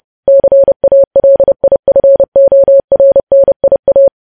Morse code, heard as a keyed tone, about 30 wpm, 550 Hz. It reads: CALIFORNIA